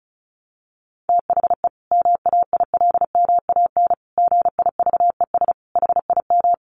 Morse code, heard as a keyed tone, about 34 wpm, 700 Hz. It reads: THE MAILMAN GIVES HIM
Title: Additional Easy Phrases 35wpm